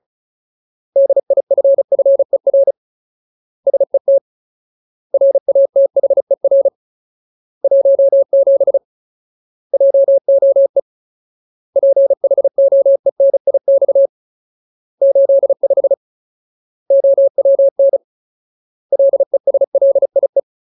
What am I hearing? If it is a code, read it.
Morse code, heard as a keyed tone, about 35 wpm, 550 Hz. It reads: DIFFER SET RATHER 17 JOE PHOENIX 85 OWN LESLIE
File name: Ultimate Word List / 35WPM Ultimate